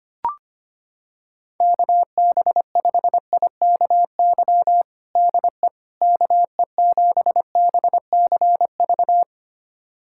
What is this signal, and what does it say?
Morse code, heard as a keyed tone, about 25 wpm, 700 Hz. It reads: KB5IKY DE KE7BCV
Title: USA Callsign DE Callsign - 6 Characters - 25wpm